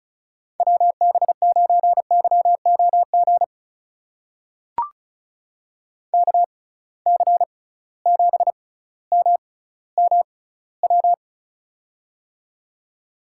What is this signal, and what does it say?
Morse code, heard as a keyed tone, about 35 wpm, spaced as for 18 wpm, 700 Hz. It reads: WB9YOG KC7MMW
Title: USA Call Signs - 6 Characters - 6x Character Spacing - 35wpm